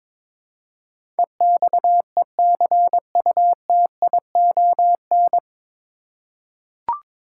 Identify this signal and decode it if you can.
Morse code, heard as a keyed tone, about 21 wpm, 700 Hz. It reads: EXECUTION
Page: Morse Code Ninja practice 22wpm